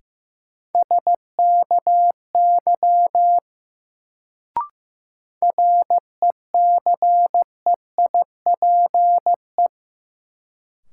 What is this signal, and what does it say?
Morse code, heard as a keyed tone, about 15 wpm, 700 Hz. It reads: SKY RECEIPE
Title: Sentences from Top 1200 Words - New Word Drill - 15wpm